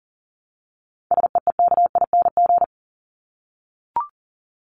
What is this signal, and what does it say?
Morse code, heard as a keyed tone, about 40 wpm, 700 Hz. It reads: SEEKING
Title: ING Suffix - Top 100 Words with Base Words 40wpm